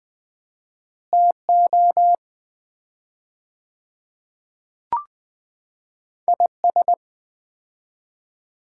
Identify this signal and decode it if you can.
Morse code, heard as a keyed tone, about 20 wpm, 700 Hz. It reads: TO IS
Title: 4x Word-Spacing - Sets of 4 Words - 20wpm